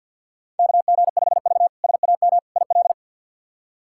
Morse code, heard as a keyed tone, about 50 wpm, 700 Hz. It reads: KG5V SAM IL
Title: K1USN SST QSO Practice - 50wpm